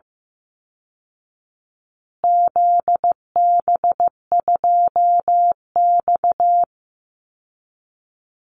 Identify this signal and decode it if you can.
Morse code, heard as a keyed tone, about 15 wpm, 700 Hz. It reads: ZB2X
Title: Call Signs Repeated 2x - 15wpm